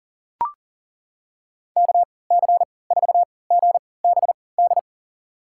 Morse code, heard as a keyed tone, about 40 wpm, spaced as for 23 wpm, 700 Hz. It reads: KC4GBD
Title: USA Call Signs - 6 Characters - 3x Character Spacing - 40wpm